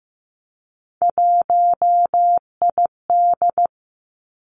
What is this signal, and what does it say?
Morse code, heard as a keyed tone, about 15 wpm, 700 Hz. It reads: 1ID
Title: Three Challenging Letter-Number Combinations 15wpm